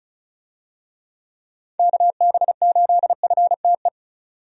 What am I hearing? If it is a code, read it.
Morse code, heard as a keyed tone, about 36 wpm, 700 Hz. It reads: KB8FTE
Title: USA Call Signs - 6 Characters - 35wpm